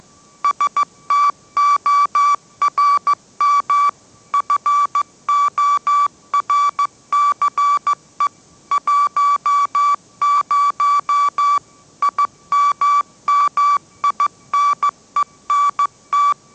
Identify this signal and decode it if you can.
Morse code, heard as a keyed tone, about 14 wpm, 1150 Hz. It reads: STORM FORCE 10 IMMINENT
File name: mrx / mrx066 16wpm